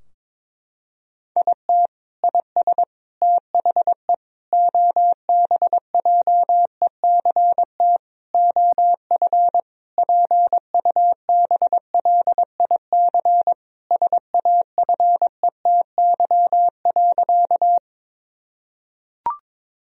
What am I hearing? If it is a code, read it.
Morse code, heard as a keyed tone, about 22 wpm, 700 Hz. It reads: IT IS THE OBJECT OF PUBLIC SAFETY.
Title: Sentences from Top 1000 Words - Review Part 4 - 22wpm